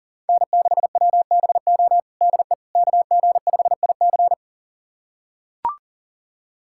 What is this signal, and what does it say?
Morse code, heard as a keyed tone, about 37 wpm, 690 Hz. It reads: N6WBO DE KG5IC
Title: USA Callsign DE Callsign - 5 Characters - 40wpm